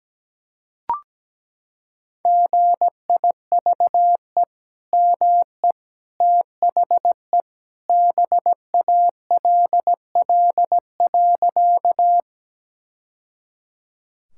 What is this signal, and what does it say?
Morse code, heard as a keyed tone, about 17 wpm, 700 Hz. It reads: GIVE ME THE BALL.
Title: Sentences from Top 1000 Words - Review Part 6 - 17wpm